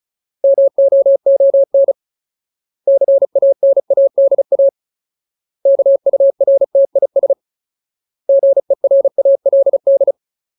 Morse code, heard as a keyed tone, about 34 wpm, 550 Hz. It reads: MOON CANADA KURTIS GERALD
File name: Ultimate Word List / 35WPM Ultimate